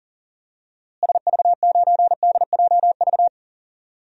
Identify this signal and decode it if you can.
Morse code, heard as a keyed tone, about 40 wpm, 700 Hz. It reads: SV9DJV
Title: DXCC-100 Call Signs - 6 Characters - Repeated 2x - 40wpm